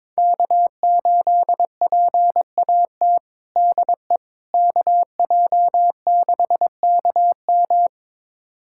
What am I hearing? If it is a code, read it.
Morse code, heard as a keyed tone, about 22 wpm, 700 Hz. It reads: K8PAT DE KJ6KM
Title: USA Callsign DE Callsign - 5 Characters - 22wpm